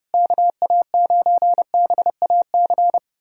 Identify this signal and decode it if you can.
Morse code, heard as a keyed tone, about 30 wpm, 700 Hz. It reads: KA9BAC
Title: USA Callsign DE Callsign - 6 Characters - 30wpm